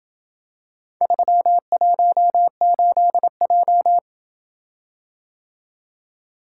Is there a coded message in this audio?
Yes: Morse code from 1.0 to 4.0 s, heard as a keyed tone, about 26 wpm, 700 Hz. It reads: 318J